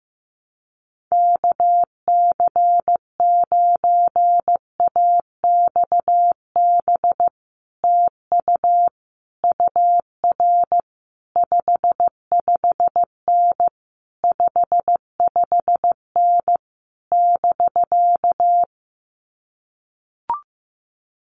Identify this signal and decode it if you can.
Morse code, heard as a keyed tone, about 15 wpm, 700 Hz. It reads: KC9AXB TU UR 55N 55N <BK>
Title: Common QSO Phrases Encore 15wpm